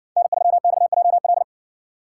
Morse code, heard as a keyed tone, about 56 wpm, 700 Hz. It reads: N3XJL